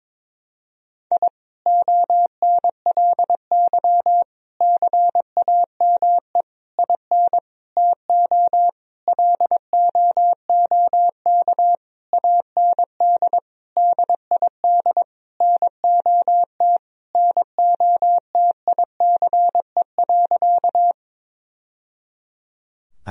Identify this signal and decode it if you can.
Morse code, heard as a keyed tone, about 22 wpm, 700 Hz. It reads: I ONLY CAME IN TO LOOK AND DID NOT NOTICE.
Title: Sentences from Top 1000 Words - Review Part 2 - 22wpm